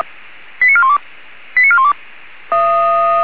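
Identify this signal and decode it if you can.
Five-tone selective call (ZVEI), heard as quick runs of five tones, tones of about 70 ms. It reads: 89412 89412
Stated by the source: ZVEI five-tone selective call received on SDR, channel 464, code 89418.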